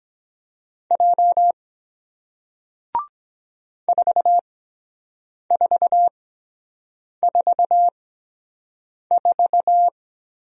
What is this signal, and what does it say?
Morse code, heard as a keyed tone, about 23 wpm, spaced as for 7 wpm, 700 Hz. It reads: J 4444